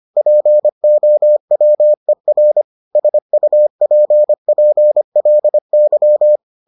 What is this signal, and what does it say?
Morse code, heard as a keyed tone, about 25 wpm, 600 Hz. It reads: POWER SUPPLY